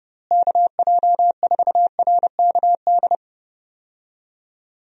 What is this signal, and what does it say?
Morse code, heard as a keyed tone, about 32 wpm, 700 Hz. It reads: KJ4RKD
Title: USA Call Signs - 6 Characters - Repeated 3x - 30wpm